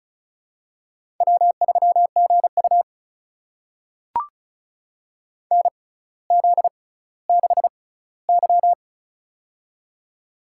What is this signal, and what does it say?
Morse code, heard as a keyed tone, about 36 wpm, spaced as for 18 wpm, 700 Hz. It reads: W3GU NZ6Y
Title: USA Call Signs - 4 Characters - 6x Character Spacing - 35wpm